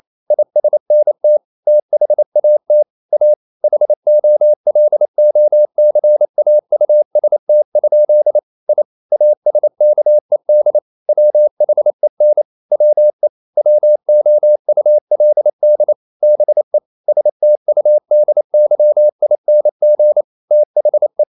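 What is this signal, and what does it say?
Morse code, heard as a keyed tone, about 28 wpm, 600 Hz. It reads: ISNT THAT A HOLOCAUST? I ASKED WHEN WE WOULD BE STUDYING THE
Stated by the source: Morse code practice at 28 wpm, text of Ken Follett's novel Never in English, from CW med Gnister.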